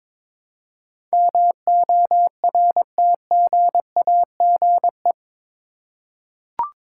Morse code, heard as a keyed tone, about 21 wpm, 700 Hz. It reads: MORTGAGE
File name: practice / 22wpm / Eight Letter Words - 22wpm